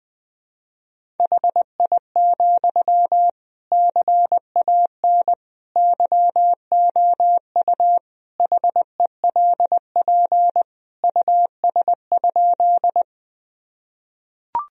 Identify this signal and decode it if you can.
Morse code, heard as a keyed tone, about 20 wpm, 700 Hz. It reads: HI, CAN YOU HELP US?
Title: Sentences from Top 1000 Words - Review Part 3 - 20wpm